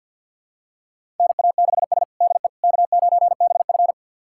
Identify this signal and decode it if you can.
Morse code, heard as a keyed tone, about 52 wpm, 700 Hz. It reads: NA6S DE K9BF
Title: USA Callsign DE Callsign - 4 Characters - 50wpm